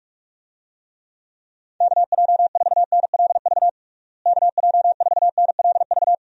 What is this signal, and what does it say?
Morse code, heard as a keyed tone, about 46 wpm, 700 Hz. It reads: KJ4NLV KJ4NLV
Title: USA Call Signs - 6 Characters - Repeated 2x - 45wpm